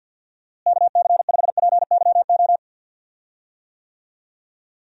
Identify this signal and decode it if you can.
Morse code, heard as a keyed tone, about 52 wpm, 700 Hz. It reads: KC5PYO
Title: USA Call Signs - 6 Characters - Repeated 3x - 50wpm